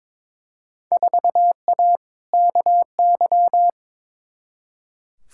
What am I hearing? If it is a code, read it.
Morse code, heard as a keyed tone, about 22 wpm, 700 Hz. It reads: 4A KY
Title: Field Day Exchanges Encore - 22wpm